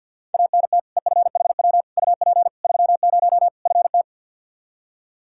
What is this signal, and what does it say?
Morse code, heard as a keyed tone, about 52 wpm, 700 Hz. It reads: ANT EFHW UP 30 FT
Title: Practice QSO - Exchange 2 - Info - 50wpm